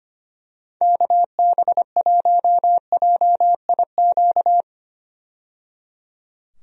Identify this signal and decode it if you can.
Morse code, heard as a keyed tone, about 25 wpm, 700 Hz. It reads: KB1JIQ
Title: USA Call Signs - 6 Characters - 25wpm